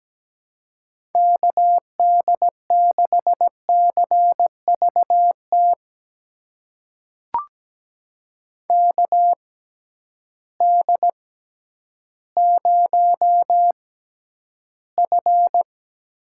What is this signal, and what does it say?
Morse code, heard as a keyed tone, about 17 wpm, spaced as for 10 wpm, 700 Hz. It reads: KD6CVT KD0F
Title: USA Call Signs - 6 Characters - 6x Character Spacing - 17wpm